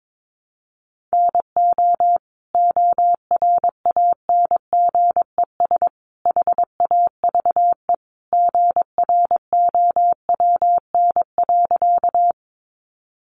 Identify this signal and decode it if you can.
Morse code, heard as a keyed tone, about 22 wpm, 700 Hz. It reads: NO ORANGES HAVE GROWN.
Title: Easy Phrases 22wpm